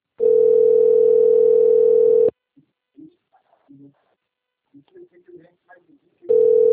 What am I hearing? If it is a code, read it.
Ringback tone (North America), 440 + 480 Hz, with speech.
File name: rg-1600-unknown- - - .38749